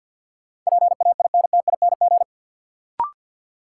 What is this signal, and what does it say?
Morse code, heard as a keyed tone, about 50 wpm, 700 Hz. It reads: PAINTING
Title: Sentences from Top 1000 Words - Review Part 6 - New Word Drill - 50wpm